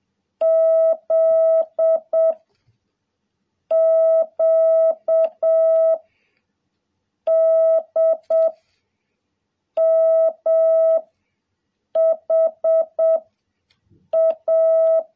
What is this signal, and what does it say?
Morse code, heard as a keyed tone, about 7 wpm, 660 Hz. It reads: Z Q D M H A
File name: jh8cbh / 2021cw